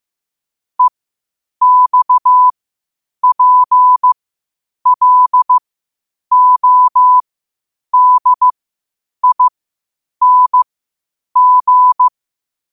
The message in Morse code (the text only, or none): EXPLODING